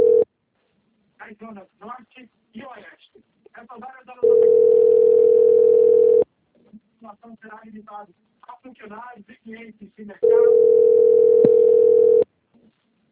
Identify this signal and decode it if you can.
Ringback tone (North America), 440 + 480 Hz, with speech.
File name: out- - - - - .3772